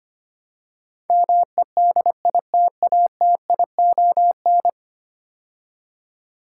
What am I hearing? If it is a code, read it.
Morse code, heard as a keyed tone, about 25 wpm, 700 Hz. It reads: MEDITATION
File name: 10 Letter Words - First 3 Letters Given - 25wpm